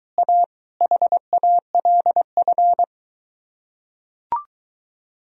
Morse code, heard as a keyed tone, about 24 wpm, 700 Hz. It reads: A HALF